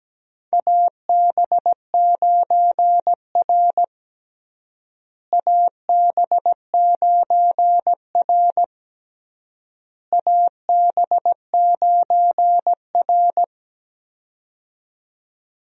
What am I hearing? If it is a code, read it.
Morse code, heard as a keyed tone, about 17 wpm, 700 Hz. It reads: AB9R AB9R AB9R
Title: USA Call Signs - 4 Characters - Repeated 3x - 17wpm